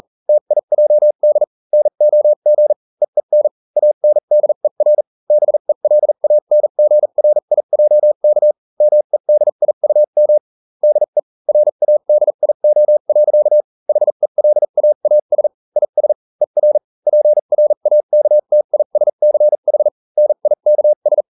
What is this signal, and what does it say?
Morse code, heard as a keyed tone, about 40 wpm, spaced as for 35 wpm, 600 Hz. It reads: TIJD NOG EEN ANDER BELANGRIJK MEDIUM DE RADIO. HELAAS IS ER PRAKTISCH NIKS